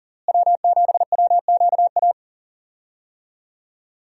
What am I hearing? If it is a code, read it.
Morse code, heard as a keyed tone, about 37 wpm, 700 Hz. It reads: W7WQA
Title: USA Call Signs - 5 Characters - Repeated 3x - 40wpm